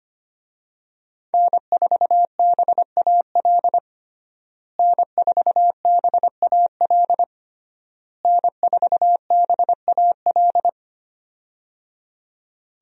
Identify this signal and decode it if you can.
Morse code, heard as a keyed tone, about 25 wpm, 700 Hz. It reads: N4BAL N4BAL N4BAL
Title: USA Call Signs - 5 Characters - Repeated 3x - 25wpm